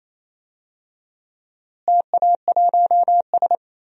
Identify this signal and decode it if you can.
Morse code, heard as a keyed tone, about 28 wpm, 700 Hz. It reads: TA1S
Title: DXCC-100 Call Signs - 4 Characters - Repeated 3x - 28wpm